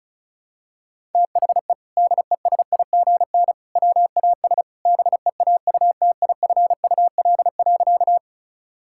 Morse code, heard as a keyed tone, about 34 wpm, 700 Hz. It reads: THE DESIGN WAS BEAUTIFUL.
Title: Sentences from Top 1000 Words - Review Part 5 - 35wpm